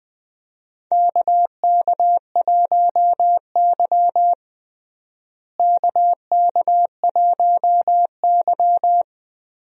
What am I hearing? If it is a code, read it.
Morse code, heard as a keyed tone, about 21 wpm, 700 Hz. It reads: KK1Y KK1Y
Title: USA Call Signs - 4 Characters - Repeated 2x - 20wpm